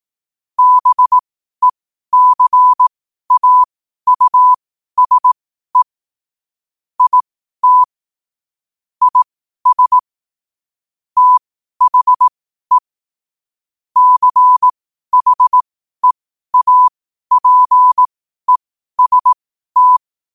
Morse code, heard as a keyed tone, about 18 wpm, spaced as for 12 wpm, 1000 Hz. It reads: BECAUSE IT IS THE CHEAPEST
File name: qotdfiles / 12wpm